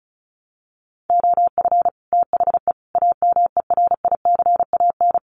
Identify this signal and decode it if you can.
Morse code, heard as a keyed tone, about 34 wpm, 700 Hz. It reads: OF THE AMERICAN
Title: Sets of 3 Words 35wpm